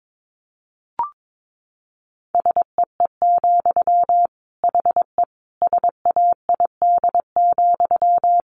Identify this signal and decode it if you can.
Morse code, heard as a keyed tone, about 22 wpm, 700 Hz. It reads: SEE, HE SAID,
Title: Sentences from Top 700 Words 22wpm